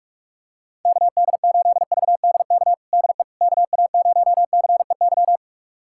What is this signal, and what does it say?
Morse code, heard as a keyed tone, about 46 wpm, 700 Hz. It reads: KD8VDK DE KA0CEY
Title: USA Callsign DE Callsign 45wpm